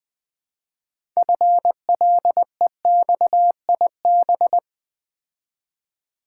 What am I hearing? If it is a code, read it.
Morse code, heard as a keyed tone, about 20 wpm, 700 Hz. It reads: FLEXIB
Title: Build A Word - 3 Letters At A Time - 8-Letter Words - 20wpm